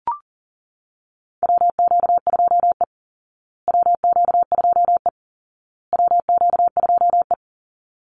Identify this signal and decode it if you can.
Morse code, heard as a keyed tone, about 40 wpm, 700 Hz. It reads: WQ2E WQ2E WQ2E
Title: Call Signs Repeated 2x - 40wpm